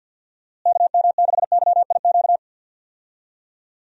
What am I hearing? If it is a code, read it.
Morse code, heard as a keyed tone, about 52 wpm, 700 Hz. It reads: KM6YIQ
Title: USA Call Signs - 6 Characters - Repeated 2x - 50wpm